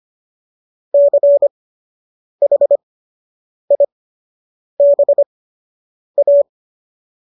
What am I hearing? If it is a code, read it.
Morse code, heard as a keyed tone, about 25 wpm, spaced as for 8 wpm, 570 Hz. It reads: CHIBA